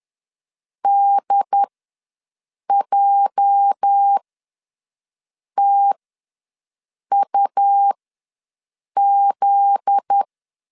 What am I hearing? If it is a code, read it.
Morse code, heard as a keyed tone, about 10 wpm, spaced as for 5 wpm, 800 Hz. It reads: DJTUZ
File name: LEZIONI / 9 28 ws 10wpm 4124